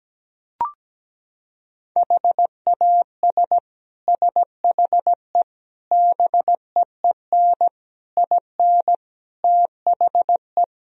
Morse code, heard as a keyed tone, about 17 wpm, 700 Hz. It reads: HAS SHE BEEN IN THE
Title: Sentences from Top 1000 Words - Review Part 4 - 17wpm